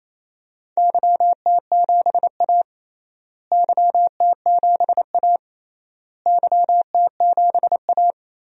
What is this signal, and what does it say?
Morse code, heard as a keyed tone, about 28 wpm, 700 Hz. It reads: YT7A YT7A YT7A
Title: DXCC-100 Call Signs - 4 Characters - Repeated 3x - 28wpm